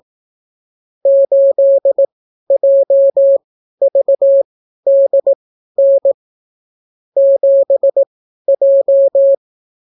Morse code, heard as a keyed tone, about 18 wpm, spaced as for 12 wpm, 550 Hz. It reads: 8JVDN 7J